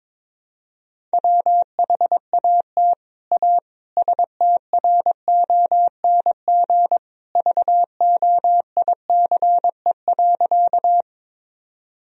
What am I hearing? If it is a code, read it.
Morse code, heard as a keyed tone, about 22 wpm, 700 Hz. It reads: WHAT A STRONG VOICE.
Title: Sentences from Top 1000 Words - Review Part 3 - 22wpm